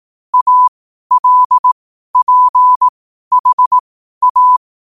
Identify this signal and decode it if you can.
Morse code, heard as a keyed tone, about 18 wpm, spaced as for 12 wpm, 1000 Hz. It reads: ALPHA
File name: qotdfiles / 12wpm